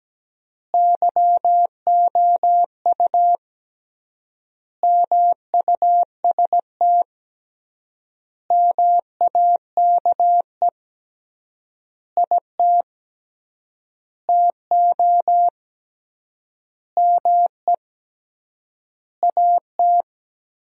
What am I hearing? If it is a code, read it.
Morse code, heard as a keyed tone, about 17 wpm, 700 Hz. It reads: YOU MUST MAKE IT TO ME AT
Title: 3x Word-Spacing - Sentences from Top 500 Words - 17wpm